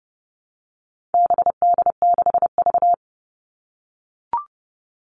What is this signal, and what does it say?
Morse code, heard as a keyed tone, about 30 wpm, 700 Hz. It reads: BD6V
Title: All Cut Numbers - 4-Digit 30wpm